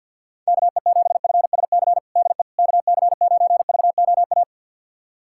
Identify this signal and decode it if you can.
Morse code, heard as a keyed tone, about 52 wpm, 690 Hz. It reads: KE7FSC DE KC9VOA
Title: USA Callsign DE Callsign - 6 Characters - 50wpm